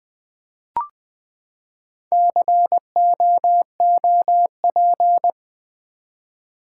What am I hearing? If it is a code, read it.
Morse code, heard as a keyed tone, about 20 wpm, 700 Hz. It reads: COOP
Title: Build A Word - 4 Letters At A Time - 11-Letter Words - 20wpm